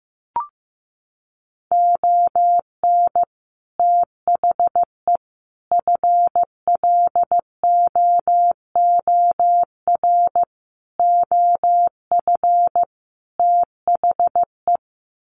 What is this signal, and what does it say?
Morse code, heard as a keyed tone, about 15 wpm, 700 Hz. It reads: ON THE FLOOR OF THE
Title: Sets of 5 Words 15wpm